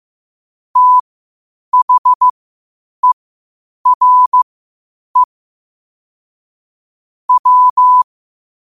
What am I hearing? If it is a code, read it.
Morse code, heard as a keyed tone, about 15 wpm, spaced as for 8 wpm, 1000 Hz. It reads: THERE W
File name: qotdfiles / 8wpm